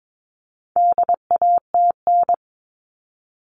Morse code, heard as a keyed tone, about 22 wpm, 700 Hz. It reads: DATN
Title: All Cut Numbers - 4-Digit 22wpm